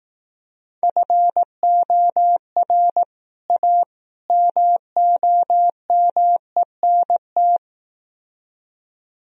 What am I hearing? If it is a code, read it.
Morse code, heard as a keyed tone, about 17 wpm, 700 Hz. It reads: FOR A MOMENT
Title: Speed-Racing - Sets of 3 Words 15wpm